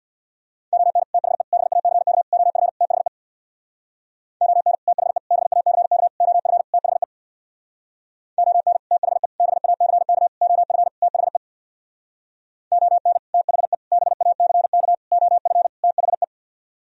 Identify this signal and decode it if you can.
Morse code, heard as a keyed tone, about 64 wpm, 690 Hz. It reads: ON THE BACK OF THE ON THE BACK OF THE ON THE BACK OF THE ON THE BACK OF THE